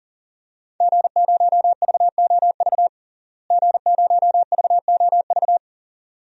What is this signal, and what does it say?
Morse code, heard as a keyed tone, about 40 wpm, 700 Hz. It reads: G0VOV G0VOV
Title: DXCC-100 Call Signs - 5 Characters - Repeated 2x - 40wpm